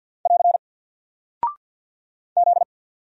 Morse code, heard as a keyed tone, about 48 wpm, 700 Hz. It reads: <AR> Z
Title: Single Character (Letter:Number:Punctuation:Prosign) Alt 50wpm